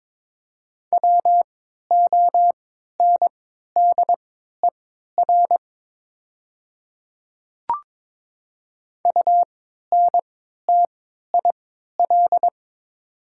Morse code, heard as a keyed tone, about 22 wpm, spaced as for 12 wpm, 700 Hz. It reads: WONDER UNTIL
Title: 3x Character Spacing - Top 500 Words - 22wpm